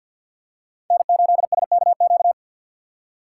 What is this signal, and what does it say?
Morse code, heard as a keyed tone, about 53 wpm, 700 Hz. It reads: N8SKQ
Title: USA Call Signs - 5 Characters - 2x Character Spacing - 50wpm